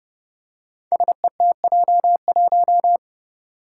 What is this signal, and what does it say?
Morse code, heard as a keyed tone, about 30 wpm, 700 Hz. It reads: SETJ1
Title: Five Challenging Letter-Number Combinations 30wpm